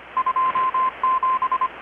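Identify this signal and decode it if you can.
Morse code, heard as a keyed tone, about 25 wpm, 1040 Hz, with radio noise. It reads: 27
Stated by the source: Morse from KPH's Night of Nights XX maritime radiotelegraphy event, received in Chiapas, Mexico about 3,800 km away on a Kenwood R-600 with a grounded 10 m long-wire antenna.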